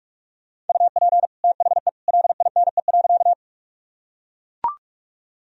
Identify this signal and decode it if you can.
Morse code, heard as a keyed tone, about 41 wpm, 700 Hz. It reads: UP THE LINE.
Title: Sentences from Top 1000 Words - Review Part 1 - 45wpm